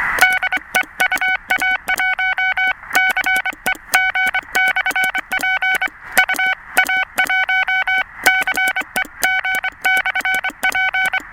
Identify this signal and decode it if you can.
Morse code, heard as a keyed tone, about 24 wpm, 2250 Hz, with noise. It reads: DE UA1CEG/P UA1CEG/P